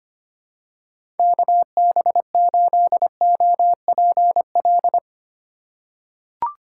Morse code, heard as a keyed tone, about 25 wpm, 700 Hz. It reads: KB8OPL
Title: USA Call Signs - 6 Characters - 4x Character Spacing - 25wpm